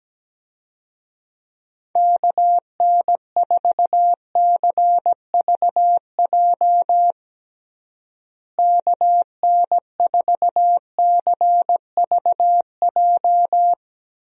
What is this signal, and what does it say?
Morse code, heard as a keyed tone, about 17 wpm, 700 Hz. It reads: KN4CVJ KN4CVJ
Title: USA Call Signs - 6 Characters - Repeated 2x - 17wpm